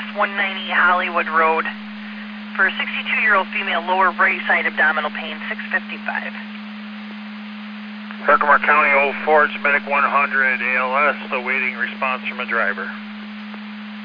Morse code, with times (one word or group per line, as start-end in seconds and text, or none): none